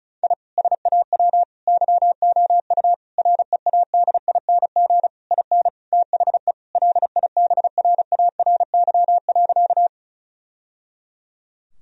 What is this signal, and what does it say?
Morse code, heard as a keyed tone, about 35 wpm, 700 Hz. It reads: I SAW YOU READING IN THE LIBRARY.
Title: Sentences from Top 1000 Words - Review Part 4 - 35wpm